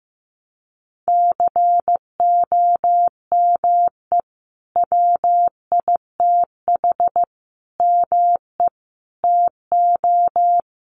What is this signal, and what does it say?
Morse code, heard as a keyed tone, about 15 wpm, 700 Hz. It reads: COME WITH ME TO
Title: Sentences from Top 200 Words 15wpm